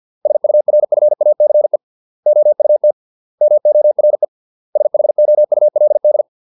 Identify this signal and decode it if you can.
Morse code, heard as a keyed tone, about 50 wpm, 600 Hz. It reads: SURFACE OUT MORE SHOULD